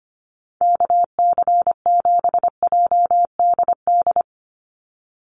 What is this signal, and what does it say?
Morse code, heard as a keyed tone, about 25 wpm, 700 Hz. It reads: KC7JDD
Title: USA Call Signs 25wpm